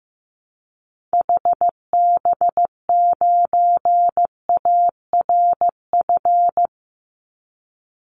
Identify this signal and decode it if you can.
Morse code, heard as a keyed tone, about 15 wpm, 700 Hz. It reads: HB9ARF
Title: Call Signs Repeated 3x - 15wpm